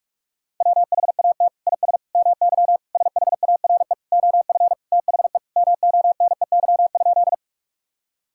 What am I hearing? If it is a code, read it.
Morse code, heard as a keyed tone, about 41 wpm, 700 Hz. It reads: WHAT IS MY SHARE OF THE MONEY?
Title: Sentences from Top 1000 Words - Review Part 6 - 45wpm